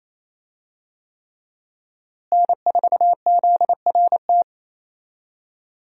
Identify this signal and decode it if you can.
Morse code, heard as a keyed tone, about 28 wpm, 700 Hz. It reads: N4ZRT
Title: USA Call Signs - 5 Characters - Repeated 3x - 28wpm